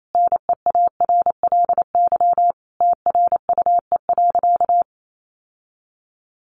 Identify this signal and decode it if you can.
Morse code, heard as a keyed tone, about 28 wpm, 700 Hz. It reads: NEARLY TRUE.
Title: Sentences from Top 1300 Words 28wpm